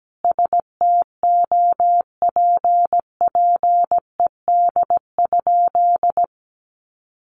Morse code, heard as a keyed tone, about 17 wpm, 700 Hz. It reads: STOPPED?
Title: A Sound Of Thunder 17wpm